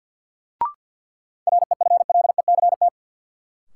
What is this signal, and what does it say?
Morse code, heard as a keyed tone, about 53 wpm, 700 Hz. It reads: REFLECT
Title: Sentences from Top 1500 Words - New Word Drill - 50wpm